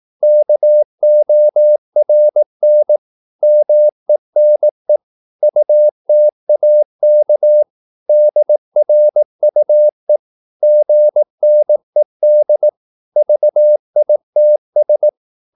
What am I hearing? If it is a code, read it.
Morse code, heard as a keyed tone, about 18 wpm, 600 Hz. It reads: KORN MENE UTAK DRUE GNED VITS
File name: Korte ord DK 18wpm